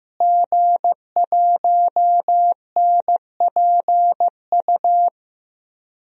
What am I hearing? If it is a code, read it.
Morse code, heard as a keyed tone, about 15 wpm, 700 Hz. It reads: G1NPU
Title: DXCC-100 Call Signs - 5 Characters - Repeated 2x - 15wpm